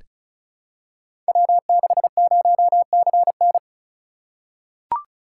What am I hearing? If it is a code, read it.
Morse code, heard as a keyed tone, about 36 wpm, 690 Hz. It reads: W60CN
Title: Code Groups - 5 Letter:Numbers - 35wpm